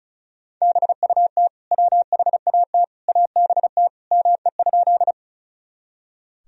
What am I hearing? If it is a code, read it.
Morse code, heard as a keyed tone, about 34 wpm, 700 Hz. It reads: BUT WHAT ABT ME?
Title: Sentences from Top 200 Words and Abbreviations 35wpm